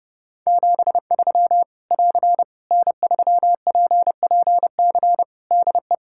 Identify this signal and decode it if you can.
Morse code, heard as a keyed tone, about 28 wpm, 700 Hz. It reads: 73 <AR> N3PPC DE
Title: Practice QSO - Exchange 4 - Ending - 30wpm